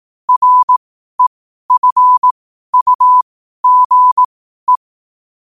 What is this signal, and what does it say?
Morse code, heard as a keyed tone, about 18 wpm, spaced as for 12 wpm, 1000 Hz. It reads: REFUGE
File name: qotdfiles / 12wpm